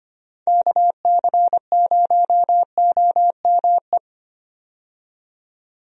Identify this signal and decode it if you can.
Morse code, heard as a keyed tone, about 25 wpm, 700 Hz. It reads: KC0OME
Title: USA Call Signs - 6 Characters - 25wpm